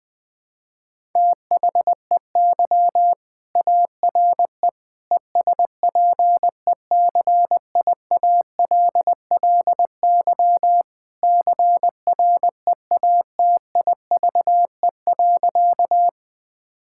Morse code, heard as a keyed tone, about 20 wpm, 700 Hz. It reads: THEY ARE ESPECIALLY CREATIVE.
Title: Sentences from Top 1000 Words - Review Part 5 - 20wpm